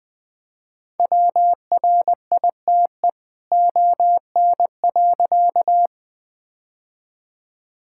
Morse code, heard as a keyed tone, about 20 wpm, 700 Hz. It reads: WRITE ON.
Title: All-Age Appropriate Jokes - 20wpm